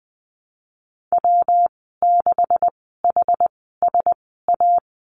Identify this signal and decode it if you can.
Morse code, heard as a keyed tone, about 20 wpm, spaced as for 15 wpm, 700 Hz. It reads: W6HSA
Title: 167 Lesson 27g - Number 7 - Call Signs 20wpm